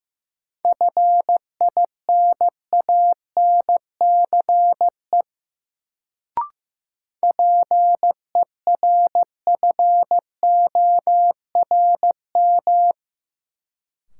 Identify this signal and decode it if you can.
Morse code, heard as a keyed tone, about 15 wpm, 700 Hz. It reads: FINANCE PERFORM
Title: Sentences from Top 1500 Words - New Word Drill Alt - 15wpm